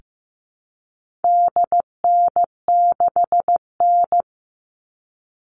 Morse code, heard as a keyed tone, about 15 wpm, 700 Hz. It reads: DN6N
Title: All Cut Numbers - 4-Digit 15wpm